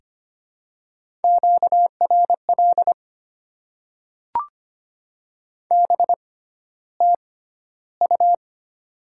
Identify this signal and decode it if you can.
Morse code, heard as a keyed tone, about 25 wpm, spaced as for 13 wpm, 700 Hz. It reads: QRL BTU
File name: Top 20 QSO Elements - 6x Character Spacing - 25wpm